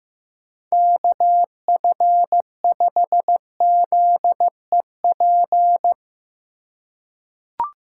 Morse code, heard as a keyed tone, about 15 wpm, 700 Hz. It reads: KF5ZEP
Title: USA Call Signs - 6 Characters - 3x Character Spacing - 15wpm